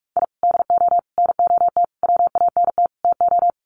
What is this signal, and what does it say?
Morse code, heard as a keyed tone, about 44 wpm, 700 Hz. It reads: I DO NOT WANT TO
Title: Sets of 5 Words Encore2 45wpm